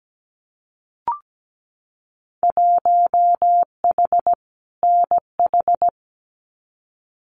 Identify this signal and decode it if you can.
Morse code, heard as a keyed tone, about 17 wpm, 700 Hz. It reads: 1H NH
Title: Winter Field Day Exchanges 17wpm